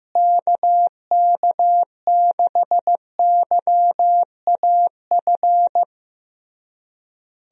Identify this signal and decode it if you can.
Morse code, heard as a keyed tone, about 16 wpm, 700 Hz. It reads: KK6YAF
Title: USA Call Signs - 6 Characters - Repeated 3x - 15wpm